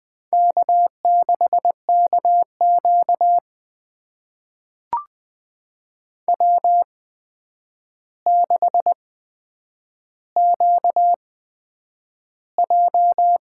Morse code, heard as a keyed tone, about 20 wpm, spaced as for 9 wpm, 700 Hz. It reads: K6KQ W6QJ
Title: USA Call Signs - 4 Characters - 8x Character Spacing - 20wpm